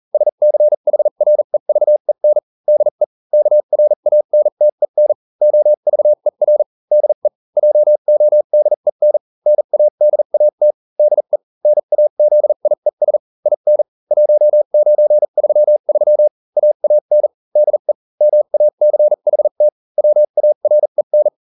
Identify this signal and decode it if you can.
Morse code, heard as a keyed tone, about 40 wpm, spaced as for 35 wpm, 600 Hz. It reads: SCHREVEN DE KRANTEN OVER DE JODEN NADAT DE NAZIES IN 1933 AAN DE MACHT WAREN